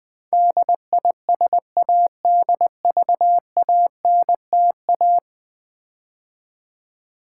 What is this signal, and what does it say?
Morse code, heard as a keyed tone, about 20 wpm, 700 Hz. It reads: DISADVANTA
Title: Build A Word - 5 Letters At A Time - 12-Letter Words - 20wpm